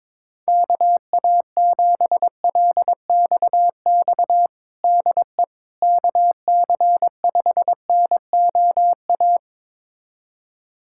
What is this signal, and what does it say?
Morse code, heard as a keyed tone, about 22 wpm, 700 Hz. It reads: KA7LXX DE KC5NOA
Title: USA Callsign DE Callsign 22wpm